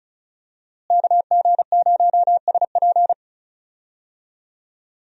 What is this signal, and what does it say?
Morse code, heard as a keyed tone, about 34 wpm, 700 Hz. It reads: KG0SP
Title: USA Call Signs - 5 Characters - Repeated 3x - 35wpm